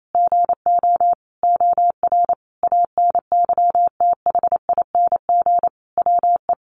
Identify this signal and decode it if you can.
Morse code, heard as a keyed tone, about 28 wpm, 700 Hz. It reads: GO OR ANYTHING WE
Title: Sentences from Top 1400 Words 28wpm